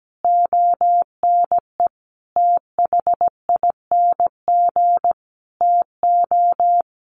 Morse code, heard as a keyed tone, about 17 wpm, 700 Hz. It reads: ONE THING TO
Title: Sentences from Top 100 Words 17wpm